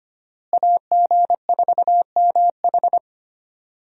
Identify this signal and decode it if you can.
Morse code, heard as a keyed tone, about 25 wpm, 700 Hz. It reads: AG4MH
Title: USA Call Signs - 5 Characters - Repeated 3x - 25wpm